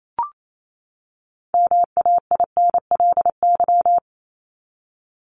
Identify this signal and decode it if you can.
Morse code, heard as a keyed tone, about 28 wpm, 700 Hz. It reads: MAINLY
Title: Top 1251-1500 Words 28wpm